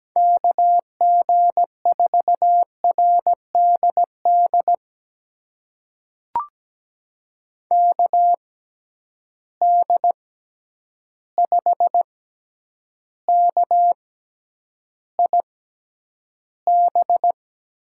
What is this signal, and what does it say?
Morse code, heard as a keyed tone, about 17 wpm, spaced as for 9 wpm, 700 Hz. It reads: KG4RDD KD5KIB
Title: USA Call Signs - 6 Characters - 6x Character Spacing - 17wpm